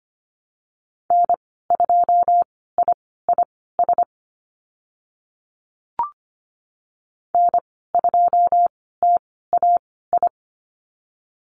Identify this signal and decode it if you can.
Morse code, heard as a keyed tone, about 25 wpm, spaced as for 16 wpm, 700 Hz. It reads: N2IIS N2TAI
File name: Lesson 15g - Number 5 - Call Signs 25wpm